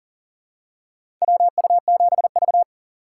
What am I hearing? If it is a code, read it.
Morse code, heard as a keyed tone, about 40 wpm, 700 Hz. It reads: WU7V
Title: USA Call Signs - 4 Characters - 4x Character Spacing - 40wpm